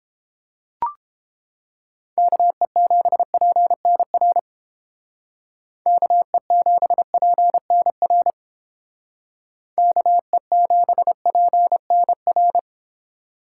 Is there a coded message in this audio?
Yes: Morse code from 2.2 to 12.6 s, heard as a keyed tone, about 29 wpm, 700 Hz. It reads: KE7PNR KE7PNR KE7PNR